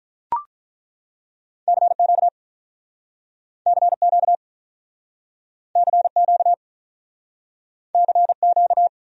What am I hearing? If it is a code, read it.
Morse code, heard as a keyed tone, about 48 wpm, 700 Hz. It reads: CQ CQ CQ CQ